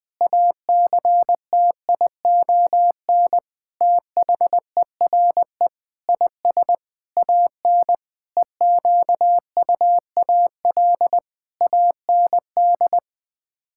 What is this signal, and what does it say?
Morse code, heard as a keyed tone, about 21 wpm, 700 Hz. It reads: ACTION THERE IS AN EQUAL AND
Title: Sets of 8 Words - 20wpm